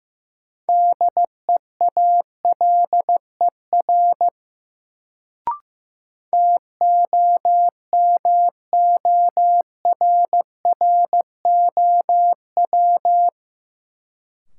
Morse code, heard as a keyed tone, about 16 wpm, 700 Hz. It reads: DEALER TOMORROW
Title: Sentences from Top 1100 Words - New Word Drill Alt - 15wpm